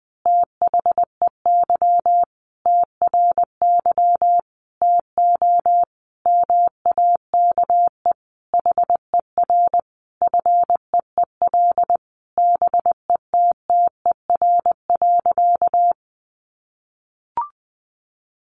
Morse code, heard as a keyed tone, about 20 wpm, 700 Hz. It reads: THEY TRY TO MAKE HER FEEL BETTER.
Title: Easy Phrases 20wpm